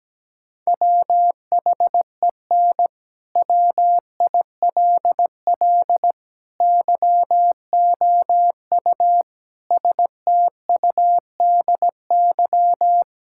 Morse code, heard as a keyed tone, about 17 wpm, 700 Hz. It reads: WHEN WILL YOU STUDY
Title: Sentences from Top 1000 Words - Review Part 6 - 17wpm